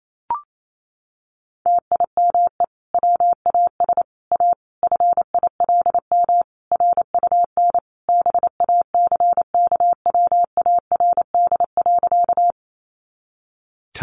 Morse code, heard as a keyed tone, about 28 wpm, 700 Hz. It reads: TIME WAS A FILM RUN BACKWARD.
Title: A Sound Of Thunder 28wpm